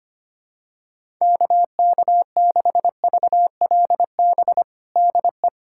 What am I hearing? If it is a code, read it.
Morse code, heard as a keyed tone, about 25 wpm, 700 Hz. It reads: KK6VLB DE
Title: USA Callsign DE Callsign - 6 Characters - 25wpm